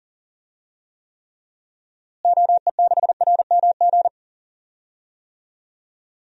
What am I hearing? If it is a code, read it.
Morse code, heard as a keyed tone, about 40 wpm, 700 Hz. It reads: OE6RMG
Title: DXCC-100 Call Signs - 6 Characters - Repeated 2x - 40wpm